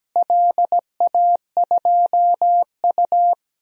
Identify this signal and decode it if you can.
Morse code, heard as a keyed tone, about 17 wpm, 700 Hz. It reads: LA2U
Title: DXCC-100 Call Signs - 4 Characters - Repeated 2x - 17wpm